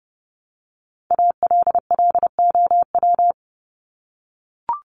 Morse code, heard as a keyed tone, about 30 wpm, 700 Hz. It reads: ALLOW
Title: Top 501-750 Words 30wpm